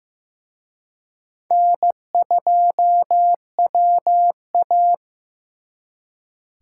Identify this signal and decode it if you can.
Morse code, heard as a keyed tone, about 18 wpm, 700 Hz. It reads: N2WA
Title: Speed Racing - USA 1x2 Call Signs 15wpm